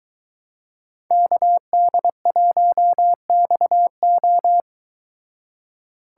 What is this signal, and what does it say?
Morse code, heard as a keyed tone, about 23 wpm, 700 Hz. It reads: KD1XO